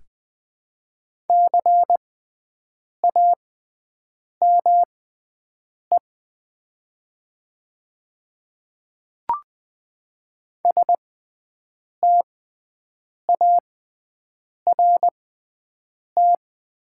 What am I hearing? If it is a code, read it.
Morse code, heard as a keyed tone, about 20 wpm, spaced as for 7 wpm, 700 Hz. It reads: CAME START
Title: 6x Character Spacing - Top 300 Words - 20wpm